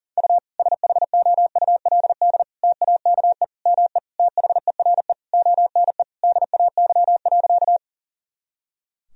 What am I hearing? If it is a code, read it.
Morse code, heard as a keyed tone, about 40 wpm, 700 Hz. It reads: U SHOULD TAKE ME THERE ONE DAY.
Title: Sentences from Top 300 Words and Abbreviations 40wpm